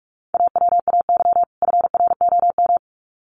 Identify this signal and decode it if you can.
Morse code, heard as a keyed tone, about 44 wpm, 700 Hz. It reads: AWAY FROM
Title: Sets of 2 Words 45wpm